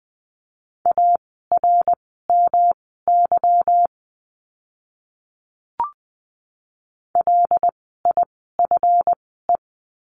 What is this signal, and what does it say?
Morse code, heard as a keyed tone, about 20 wpm, spaced as for 14 wpm, 700 Hz. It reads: ARMY LIFE
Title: 146 Lesson 24f - Letter y - Words 20wpm